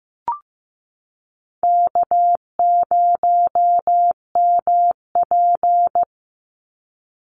Morse code, heard as a keyed tone, about 15 wpm, 700 Hz. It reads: K0MP
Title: Easy Call Signs Repeated 3x - 15wpm